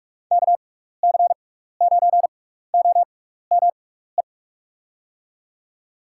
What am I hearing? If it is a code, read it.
Morse code, heard as a keyed tone, about 43 wpm, spaced as for 15 wpm, 700 Hz. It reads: KC9OME
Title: USA Call Signs - 6 Characters - 6x Character Spacing - 45wpm